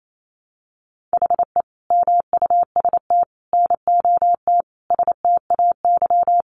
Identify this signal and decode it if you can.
Morse code, heard as a keyed tone, about 28 wpm, 700 Hz. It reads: HE MUST NOT STAY
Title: Sentences from Top 600 Words 28wpm